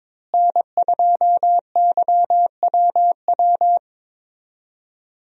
Morse code, heard as a keyed tone, about 22 wpm, 700 Hz. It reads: N2YWW
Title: USA Call Signs - 5 Characters - Repeated 2x - 22wpm